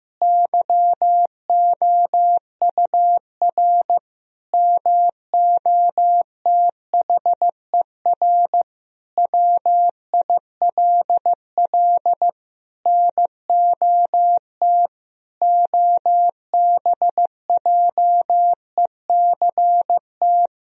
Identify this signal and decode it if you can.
Morse code, heard as a keyed tone, about 16 wpm, 700 Hz. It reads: YOUR MOTHER WILL NOT OBJECT
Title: Sentences from Top 1000 Words - Review Part 3 - 15wpm